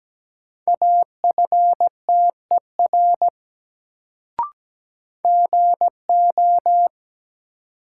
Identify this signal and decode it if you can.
Morse code, heard as a keyed tone, about 17 wpm, 700 Hz. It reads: AFTER GO
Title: Sets of 2 Words - New Word Drill - 17wpm